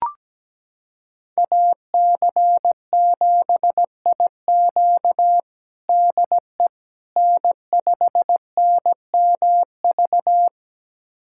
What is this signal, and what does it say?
Morse code, heard as a keyed tone, about 17 wpm, 700 Hz. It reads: AC7IQ DE N5NMV
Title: USA Callsign DE Callsign - 5 Characters - 17wpm